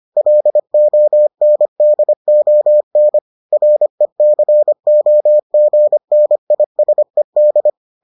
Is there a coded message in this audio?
Yes: Morse code from 0.2 to 7.7 s, heard as a keyed tone, about 25 wpm, 600 Hz. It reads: LONDON RECOGNISED